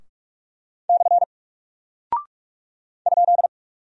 Morse code, heard as a keyed tone, about 46 wpm, 700 Hz. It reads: / ?
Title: Single Character (Letter:Number:Punctuation:Prosign) Alt 45wpm